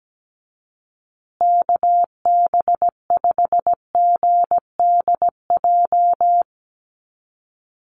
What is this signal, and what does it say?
Morse code, heard as a keyed tone, about 17 wpm, 700 Hz. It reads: KB5GDJ
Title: USA Call Signs Repeated 2x 17wpm